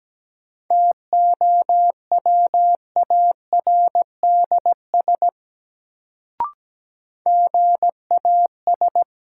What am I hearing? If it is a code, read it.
Morse code, heard as a keyed tone, about 17 wpm, 700 Hz. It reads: TOWARDS GAS
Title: Sentences from Top 1100 Words - New Word Drill Alt - 17wpm